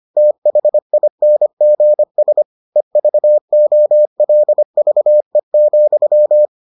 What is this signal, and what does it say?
Morse code, heard as a keyed tone, about 25 wpm, 600 Hz. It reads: THINGS EVOLVE,